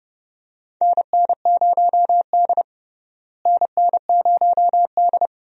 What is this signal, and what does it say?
Morse code, heard as a keyed tone, about 31 wpm, 700 Hz. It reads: NN0D NN0D
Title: USA Call Signs - 4 Characters - Repeated 2x - 30wpm